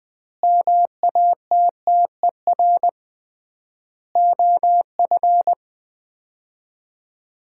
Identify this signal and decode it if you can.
Morse code, heard as a keyed tone, about 21 wpm, 700 Hz. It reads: MATTER OF
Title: 3x Word-Spacing - Sets of 3 Words - 20wpm